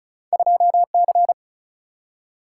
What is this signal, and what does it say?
Morse code, heard as a keyed tone, about 36 wpm, 700 Hz. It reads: 2C
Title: Single-Letter Single-Number 35wpm